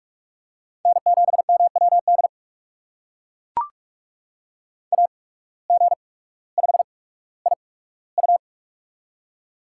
Morse code, heard as a keyed tone, about 41 wpm, spaced as for 19 wpm, 700 Hz. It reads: N7MWD AG5IU
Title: USA Call Signs - 5 Characters - 8x Character Spacing - 45wpm